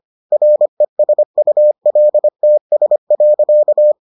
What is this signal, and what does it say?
Morse code, heard as a keyed tone, about 25 wpm, 600 Hz. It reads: RESULTS.